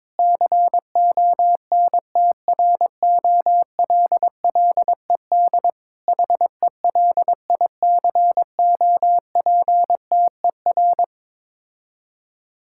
Morse code, heard as a keyed tone, about 22 wpm, 700 Hz. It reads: CONTROLLED HELICOPTER
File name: Difficult Words with Hints - 22wpm